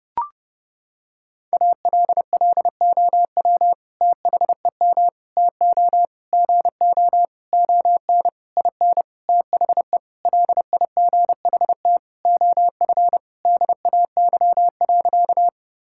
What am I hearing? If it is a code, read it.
Morse code, heard as a keyed tone, about 31 wpm, 700 Hz. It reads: ALLOW THEM TO GO ON IN THE LIGHT OF DAY.
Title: Sentences from Top 1000 Words - Review Part 5 - 30wpm